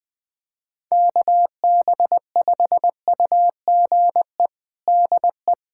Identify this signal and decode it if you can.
Morse code, heard as a keyed tone, about 21 wpm, 700 Hz. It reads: KB5UGE DE
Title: USA Callsign DE Callsign 20wpm